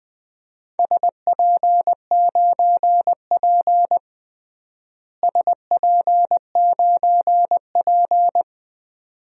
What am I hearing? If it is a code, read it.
Morse code, heard as a keyed tone, about 21 wpm, 700 Hz. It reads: SP9P SP9P
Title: DXCC-100 Call Signs - 4 Characters - Repeated 2x - 20wpm